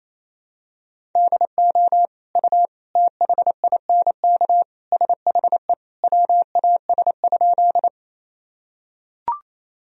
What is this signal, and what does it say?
Morse code, heard as a keyed tone, about 28 wpm, 700 Hz. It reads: DO U THINK SHE WAS?
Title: Sentences from Top 200 Words and Abbreviations 28wpm